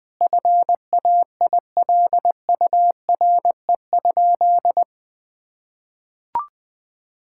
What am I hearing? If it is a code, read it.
Morse code, heard as a keyed tone, about 20 wpm, 700 Hz. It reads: FAILURE?
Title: Sentences from Top 1000 Words - Review Part 6 - 20wpm